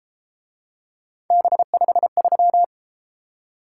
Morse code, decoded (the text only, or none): B53